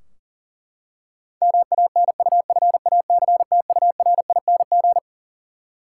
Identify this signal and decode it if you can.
Morse code, heard as a keyed tone, about 40 wpm, 700 Hz. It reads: MANUFACTURING
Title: Practice QSO - Exchange 3 - More Info - 40wpm